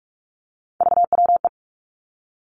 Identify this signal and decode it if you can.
Morse code, heard as a keyed tone, about 46 wpm, 700 Hz. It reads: VWE
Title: Three Challenging Letter-Number Combinations 45wpm